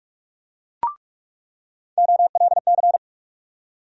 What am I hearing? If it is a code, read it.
Morse code, heard as a keyed tone, about 43 wpm, 700 Hz. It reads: OLC
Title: USA Call Sign Postfixes - 3 Letters 45wpm